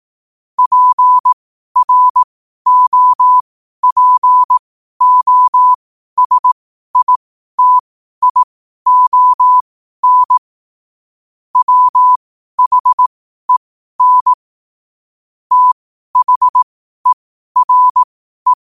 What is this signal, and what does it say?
Morse code, heard as a keyed tone, about 18 wpm, spaced as for 12 wpm, 1000 Hz. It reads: PROPOSITION WHEN THERE
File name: qotdfiles / 12wpm